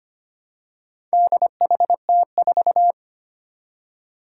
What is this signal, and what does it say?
Morse code, heard as a keyed tone, about 25 wpm, 700 Hz. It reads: DHT4
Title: Four Challenging Letter-Number Combinations 25wpm